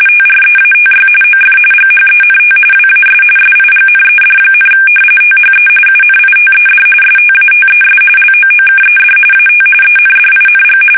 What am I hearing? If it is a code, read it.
RTTY, about 45.45 baud. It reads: TREATY OF GHENT WAS SIGNED, THE UNITED STATES LOOKED AT ENDING
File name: fsk850 / usnavy2